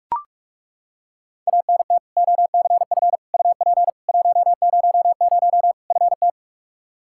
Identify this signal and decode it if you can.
Morse code, heard as a keyed tone, about 41 wpm, 700 Hz. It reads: ANT OCF UP 100 FT
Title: Practice QSO - Exchange 2 - Info - 45wpm